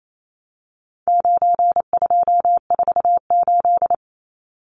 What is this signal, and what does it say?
Morse code, heard as a keyed tone, about 28 wpm, 700 Hz. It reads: 9248
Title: Numbers - 4-Digit 28wpm